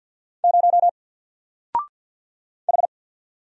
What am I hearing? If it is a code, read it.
Morse code, heard as a keyed tone, about 50 wpm, 700 Hz. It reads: 0 H
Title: Single Letter-Number Alt 50wpm